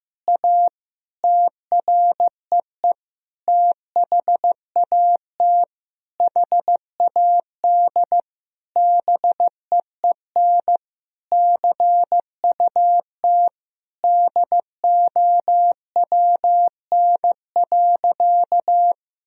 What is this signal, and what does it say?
Morse code, heard as a keyed tone, about 16 wpm, 700 Hz. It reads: A TREE THAT HAD BEEN CUT DOWN.
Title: Sentences from Top 1000 Words - Review Part 1 - 15wpm